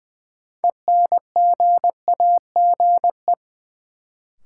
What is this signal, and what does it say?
Morse code, heard as a keyed tone, about 20 wpm, 700 Hz. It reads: ENGAGE
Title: Sentences from Top 1500 Words - New Word Drill Alt - 20wpm